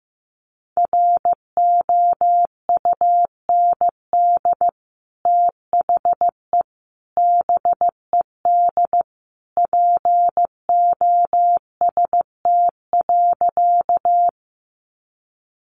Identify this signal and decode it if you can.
Morse code, heard as a keyed tone, about 15 wpm, 700 Hz. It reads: ROUND THE BED POST.
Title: Sentences from Top 1000 Words 15wpm